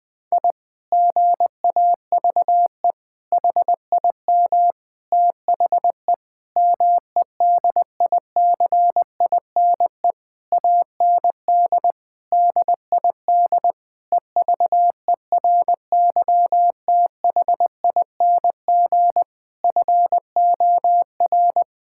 Morse code, heard as a keyed tone, about 20 wpm, 700 Hz. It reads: I GAVE HIM THE MEDICINE AND DID EVERYTHING FOR
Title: Sentences from Top 1000 Words - Review Part 6 - 20wpm